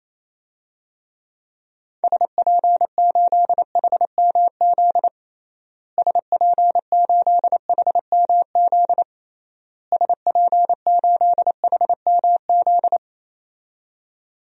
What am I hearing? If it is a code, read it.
Morse code, heard as a keyed tone, about 28 wpm, 700 Hz. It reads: SP8HMZ SP8HMZ SP8HMZ
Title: DXCC-100 Call Signs - 6 Characters - Repeated 3x - 28wpm